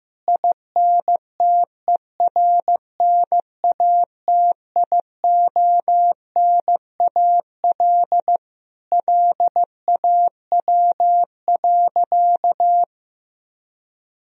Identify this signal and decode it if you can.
Morse code, heard as a keyed tone, about 15 wpm, 700 Hz. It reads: INTERNATIONAL LAW.
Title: Sentences from Top 1000 Words - Review Part 4 - 15wpm